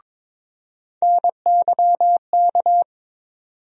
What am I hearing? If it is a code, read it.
Morse code, heard as a keyed tone, about 21 wpm, 700 Hz. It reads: NYK
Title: USA Call Sign Postfixes - 3 Letters 22wpm